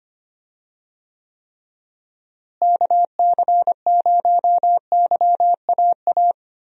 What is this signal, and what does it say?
Morse code, heard as a keyed tone, about 25 wpm, 700 Hz. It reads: KC0YAA
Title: USA Call Signs - 6 Characters - Repeated 3x - 25wpm